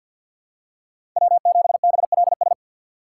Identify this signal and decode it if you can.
Morse code, heard as a keyed tone, about 49 wpm, 700 Hz. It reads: W7BLS
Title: USA Call Signs - 5 Characters - 3x Character Spacing - 50wpm